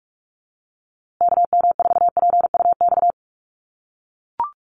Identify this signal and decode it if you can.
Morse code, heard as a keyed tone, about 44 wpm, 700 Hz. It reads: KM4PUX
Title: USA Call Signs Repeated 3x 45wpm